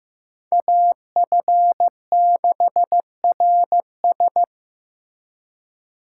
Morse code, heard as a keyed tone, about 18 wpm, 700 Hz. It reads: AF6RS
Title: Speed Racing - USA 2x2 Call Signs 15wpm